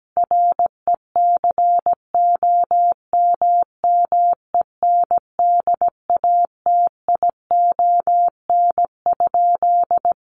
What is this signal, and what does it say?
Morse code, heard as a keyed tone, about 17 wpm, 700 Hz. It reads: RECOMMENDATION?
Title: Sentences from Top 1400 Words 17wpm